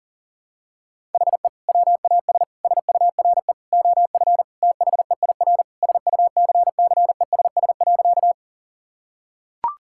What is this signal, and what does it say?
Morse code, heard as a keyed tone, about 40 wpm, 700 Hz. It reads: HE WAS SURE OF THEIR SUCCESS.
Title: Sentences from Top 1000 Words - Review Part 6 - 40wpm